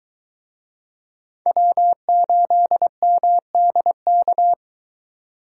Morse code, heard as a keyed tone, about 23 wpm, 700 Hz. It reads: W8MDK